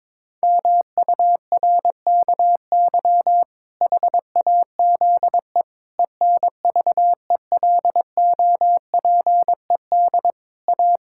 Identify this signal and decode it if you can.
Morse code, heard as a keyed tone, about 22 wpm, 700 Hz. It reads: MURKY HAZE ENVELOPED A
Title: Pangram Challenge v2 - 22wpm